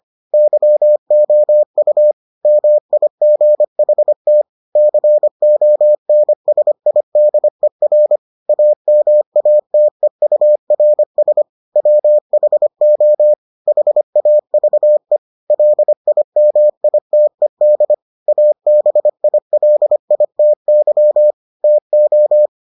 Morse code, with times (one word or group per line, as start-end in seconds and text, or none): 0.3-2.1 YOU
2.5-4.4 MIGHT
4.8-8.2 CONSIDER
8.5-11.4 AMATEURS
11.8-13.3 WHO
13.7-15.2 HAVE
15.5-17.9 LIMITED
18.3-21.3 ABILITY
21.6-22.5 TO